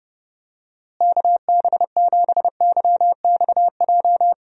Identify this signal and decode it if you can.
Morse code, heard as a keyed tone, about 30 wpm, 700 Hz. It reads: KB7YXJ
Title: USA Call Signs - 6 Characters - Repeated 2x - 30wpm